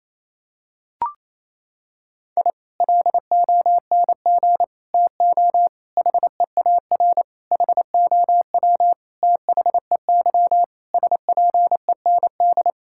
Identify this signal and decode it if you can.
Morse code, heard as a keyed tone, about 28 wpm, 700 Hz. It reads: I LONG TO HEAR HOW THEY SPEND
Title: Sentences from Top 1000 Words - Review Part 5 - 28wpm